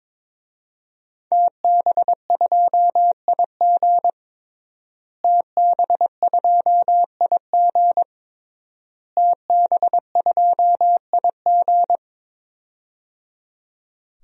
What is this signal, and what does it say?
Morse code, heard as a keyed tone, about 22 wpm, 700 Hz. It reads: TB2IG TB2IG TB2IG
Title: DXCC-100 Call Signs - 5 Characters - Repeated 3x - 22wpm